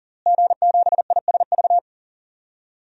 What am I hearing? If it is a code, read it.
Morse code, heard as a keyed tone, about 38 wpm, 700 Hz. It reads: G7ISV
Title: DXCC-100 Call Signs - 5 Characters - Repeated 2x - 40wpm